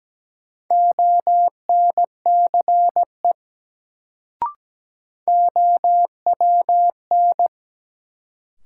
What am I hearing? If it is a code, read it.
Morse code, heard as a keyed tone, about 17 wpm, 700 Hz. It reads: ONCE OWN
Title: Sentences from Top 300 Words - New Word Drill - 17wpm